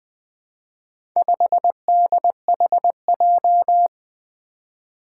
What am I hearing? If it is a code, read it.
Morse code, heard as a keyed tone, about 20 wpm, 700 Hz. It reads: 5DHJ
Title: Four Challenging Letter-Number Combinations 20wpm